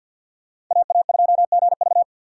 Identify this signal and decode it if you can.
Morse code, heard as a keyed tone, about 47 wpm, 700 Hz. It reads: AA2GV
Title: USA Call Signs - 5 Characters - 4x Character Spacing - 50wpm